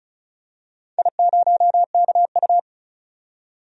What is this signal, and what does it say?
Morse code, heard as a keyed tone, about 34 wpm, 700 Hz. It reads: I0KU
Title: DXCC-100 Call Signs - 4 Characters - 35wpm